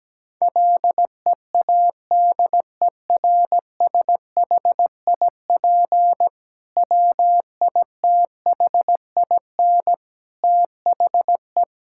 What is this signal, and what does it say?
Morse code, heard as a keyed tone, about 17 wpm, 700 Hz. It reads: LEADERSHIP WITHIN THE
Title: Sentences from Top 1600 Words 17wpm